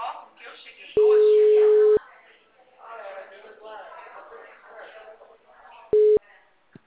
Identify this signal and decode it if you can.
Ringback tone (Europe, 425 Hz), with speech.
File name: out- -2011- - - .84